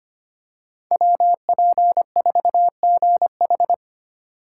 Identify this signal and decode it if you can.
Morse code, heard as a keyed tone, about 25 wpm, 700 Hz. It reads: WP4GH
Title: USA Call Signs - 5 Characters - 6x Character Spacing - 25wpm